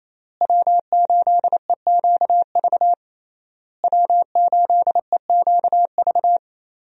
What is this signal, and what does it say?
Morse code, heard as a keyed tone, about 28 wpm, 700 Hz. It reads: W8EQV W8EQV
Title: USA Call Signs - 5 Characters - Repeated 3x - 28wpm